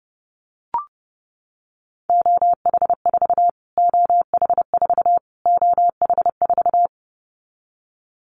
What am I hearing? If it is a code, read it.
Morse code, heard as a keyed tone, about 30 wpm, 700 Hz. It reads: OH4 OH4 OH4
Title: Callsign Prefixes for Top 25 DXCC - Repeated 3x - 30wpm